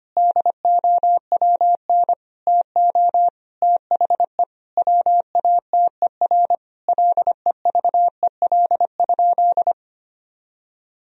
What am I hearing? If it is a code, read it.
Morse code, heard as a keyed tone, about 25 wpm, 700 Hz. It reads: DOWN TO THE WATER LEVEL?
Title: Sentences from Top 1000 Words - Review Part 4 - 25wpm